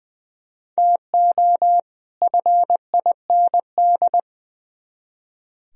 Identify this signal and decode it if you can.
Morse code, heard as a keyed tone, about 24 wpm, 700 Hz. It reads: TO FIND
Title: Speed-Racing - Sets of 2 Words 20wpm